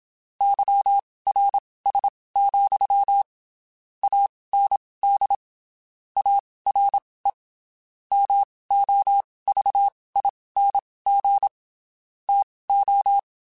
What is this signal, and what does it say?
Morse code, heard as a keyed tone, about 27 wpm, spaced as for 19 wpm, 800 Hz. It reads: YRS, AND ARE MOVING TO
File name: (spaced) / QSOs Spaced-mp3s / WD0BGP 1 @25 sp